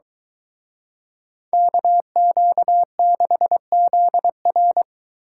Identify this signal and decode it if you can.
Morse code, heard as a keyed tone, about 23 wpm, 700 Hz. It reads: KQ6ZR